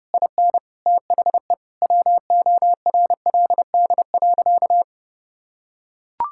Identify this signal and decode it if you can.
Morse code, heard as a keyed tone, about 30 wpm, 700 Hz. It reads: IN THE WORLD.
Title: Sentences from Top 1000 Words - Review Part 6 - 30wpm